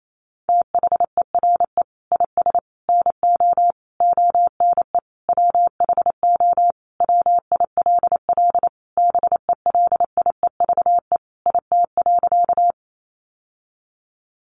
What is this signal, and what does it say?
Morse code, heard as a keyed tone, about 28 wpm, 700 Hz. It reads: THERE IS NO ONE WHO WILL BELIEVE IT.
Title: Sentences from Top 1000 Words 28wpm